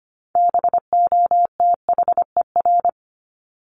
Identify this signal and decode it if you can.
Morse code, heard as a keyed tone, about 25 wpm, 700 Hz. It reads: BOTHER
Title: Top 1501-1750 Words 25wpm